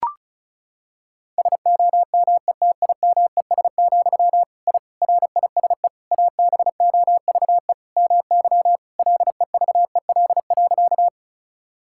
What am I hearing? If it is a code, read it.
Morse code, heard as a keyed tone, about 34 wpm, 700 Hz. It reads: SOMETIMES, I RISE ABOVE MY LEVEL.
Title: Sentences from Top 1000 Words - Review Part 6 - 35wpm